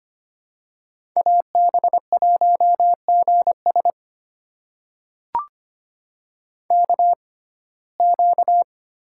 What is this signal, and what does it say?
Morse code, heard as a keyed tone, about 25 wpm, spaced as for 18 wpm, 700 Hz. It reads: AB1GS KQ
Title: USA Call Signs - 5 Characters - 6x Character Spacing - 25wpm